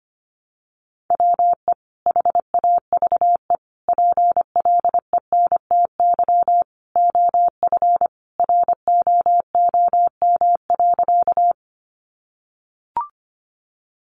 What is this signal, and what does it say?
Morse code, heard as a keyed tone, about 25 wpm, 700 Hz. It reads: WE HAVE PLENTY OF ROOM.
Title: Sentences from Top 1300 Words 25wpm